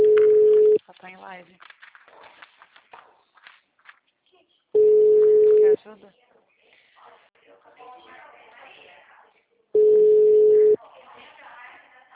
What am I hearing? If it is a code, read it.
Ringback tone (Europe, 425 Hz), with speech.